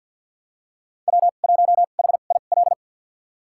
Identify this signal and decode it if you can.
Morse code, heard as a keyed tone, about 52 wpm, spaced as for 36 wpm, 700 Hz. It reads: W1HIL
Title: USA Call Signs - 5 Characters - 2x Character Spacing - 50wpm